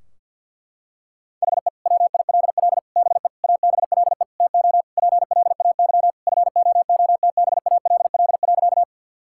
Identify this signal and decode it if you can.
Morse code, heard as a keyed tone, about 50 wpm, 700 Hz. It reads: HE WILL BE ABLE TO PLAY FOOTBALL.
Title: Sentences from Top 1600 Words 50wpm